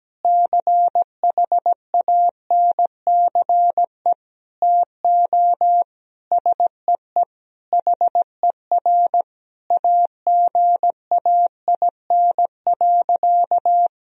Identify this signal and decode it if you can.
Morse code, heard as a keyed tone, about 17 wpm, 700 Hz. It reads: CHANCE TO SEE HER AGAIN.
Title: Sentences from Top 1000 Words - Review Part 4 - 17wpm